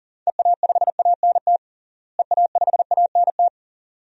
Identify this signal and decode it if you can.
Morse code, heard as a keyed tone, about 40 wpm, 700 Hz. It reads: EA5ANT EA5ANT
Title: DXCC-100 Call Signs - 6 Characters - Repeated 2x - 40wpm